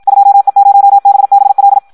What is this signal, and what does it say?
Morse code, heard as a keyed tone, about 56 wpm, 800 Hz. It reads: 2E0BBL
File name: Audio practica CW / qrq-tx-54 2